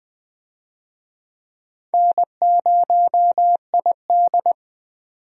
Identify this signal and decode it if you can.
Morse code, heard as a keyed tone, about 20 wpm, 700 Hz. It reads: N0ID
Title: USA Call Signs - 4 Characters - 20wpm